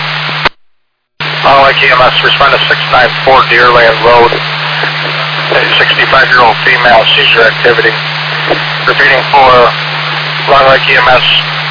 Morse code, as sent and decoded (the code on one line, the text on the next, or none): none